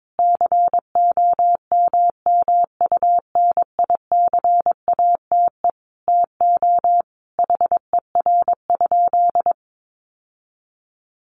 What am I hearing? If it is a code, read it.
Morse code, heard as a keyed tone, about 22 wpm, 700 Hz. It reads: COMMUNICATE TO HER?
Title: Sentences from Top 1300 Words 22wpm